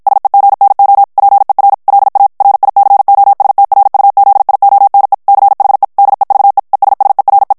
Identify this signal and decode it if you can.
Morse code, heard as a keyed tone, about 54 wpm, 800 Hz. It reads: SEGNO PER LA RICOSTRUZIONE CHE DEVE ESSERE
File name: ITALIEN texte clair 1 vitesse 50 mots minute